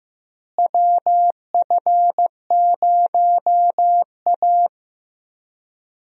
Morse code, heard as a keyed tone, about 15 wpm, 700 Hz. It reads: WF0A
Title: USA Call Signs - 4 Characters - Repeated 2x - 15wpm